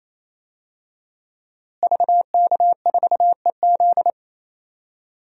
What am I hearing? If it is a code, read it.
Morse code, heard as a keyed tone, about 28 wpm, 700 Hz. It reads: VK4EZ
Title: DXCC-100 Call Signs - 5 Characters - Repeated 3x - 28wpm